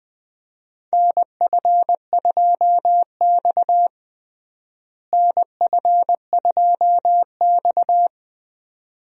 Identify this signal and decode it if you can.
Morse code, heard as a keyed tone, about 20 wpm, 700 Hz. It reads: NF2X NF2X
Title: USA Call Signs - 4 Characters - Repeated 2x - 20wpm